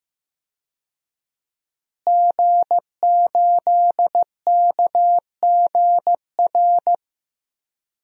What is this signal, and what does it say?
Morse code, heard as a keyed tone, about 15 wpm, 700 Hz. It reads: G8KGR
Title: DXCC-100 Call Signs - 5 Characters - Repeated 2x - 15wpm